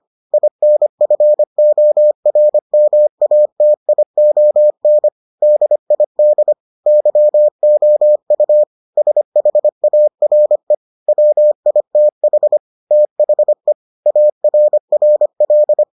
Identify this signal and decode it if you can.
Morse code, heard as a keyed tone, about 25 wpm, 600 Hz. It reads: INFORMATION DID YOU SHARE WITH THE ARRL